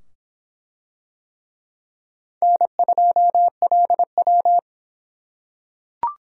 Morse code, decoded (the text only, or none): N2LW